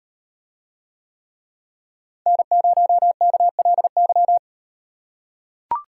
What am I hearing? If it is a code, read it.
Morse code, heard as a keyed tone, about 35 wpm, 700 Hz. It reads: N0KLY